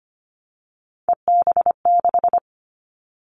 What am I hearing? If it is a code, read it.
Morse code, heard as a keyed tone, about 25 wpm, 700 Hz. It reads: EB6
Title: Callsign Prefixes for Top 100 DXCC - 25wpm